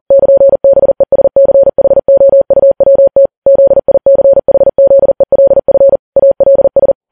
Morse code, heard as a keyed tone, about 40 wpm, 550 Hz. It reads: <KN>BESCHOUWT ZICHZELF ALS